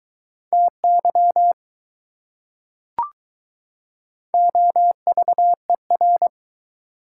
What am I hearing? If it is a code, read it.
Morse code, heard as a keyed tone, about 22 wpm, 700 Hz. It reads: TY OVER